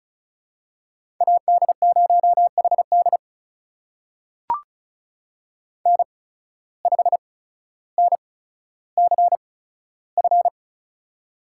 Morse code, heard as a keyed tone, about 34 wpm, spaced as for 15 wpm, 700 Hz. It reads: AD0HD N5NCF
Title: USA Call Signs - 8x Character Spacing - 35wpm